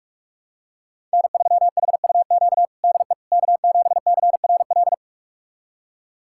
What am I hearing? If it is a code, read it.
Morse code, heard as a keyed tone, about 42 wpm, 700 Hz. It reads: N3HUQ DE K7CRL
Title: USA Callsign DE Callsign - 5 Characters - 45wpm